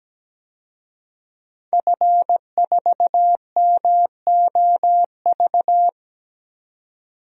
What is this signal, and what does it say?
Morse code, heard as a keyed tone, about 17 wpm, 700 Hz. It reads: F4MOV
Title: DXCC-100 Call Signs - 5 Characters - 17wpm